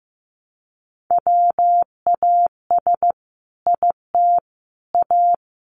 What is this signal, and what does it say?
Morse code, heard as a keyed tone, about 15 wpm, 700 Hz. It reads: WAS IT A
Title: Sentences from Top 1100 Words 15wpm